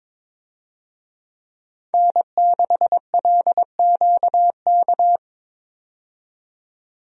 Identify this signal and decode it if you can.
Morse code, heard as a keyed tone, about 21 wpm, 700 Hz. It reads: N6LQK
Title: USA Call Signs - 5 Characters - 22wpm